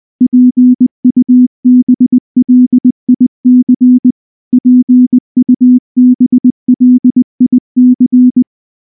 Morse code, heard as a keyed tone, about 20 wpm, 250 Hz. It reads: PUBLIC PUBLIC